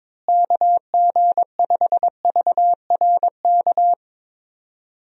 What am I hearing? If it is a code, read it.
Morse code, heard as a keyed tone, about 22 wpm, 700 Hz. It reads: KG5VRK
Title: USA Call Signs - 6 Characters - Repeated 2x - 22wpm